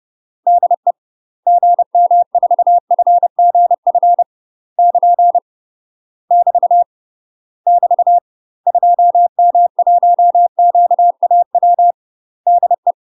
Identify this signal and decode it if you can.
Morse code, heard as a keyed tone, about 30 wpm, 700 Hz. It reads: DE GM4FGF <KN> <BT> <BT> 2M1QAW DE
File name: UK MASTER QSOs TRAINING / 30WPM / UK MASTER QSOs 0007